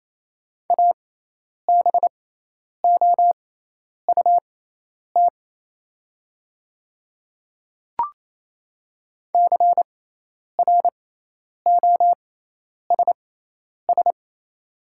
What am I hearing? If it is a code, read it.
Morse code, heard as a keyed tone, about 28 wpm, spaced as for 10 wpm, 700 Hz. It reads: ABOUT CROSS
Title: 6x Character Spacing - Top 300 Words - 28wpm